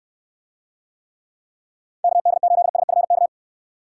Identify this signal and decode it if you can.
Morse code, heard as a keyed tone, about 69 wpm, 700 Hz. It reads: KD8SFK